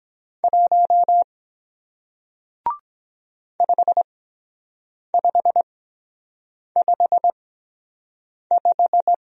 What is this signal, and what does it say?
Morse code, heard as a keyed tone, about 23 wpm, spaced as for 7 wpm, 700 Hz. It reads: 1 5555